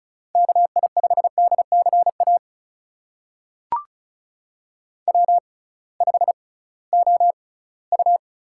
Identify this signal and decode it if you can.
Morse code, heard as a keyed tone, about 36 wpm, spaced as for 21 wpm, 700 Hz. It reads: KI5DCA W5OU
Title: USA Call Signs - 6x Character Spacing - 35wpm